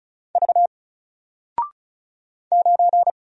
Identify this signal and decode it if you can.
Morse code, heard as a keyed tone, about 36 wpm, 700 Hz. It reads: V 9
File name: Single-Letter-Number-35wpm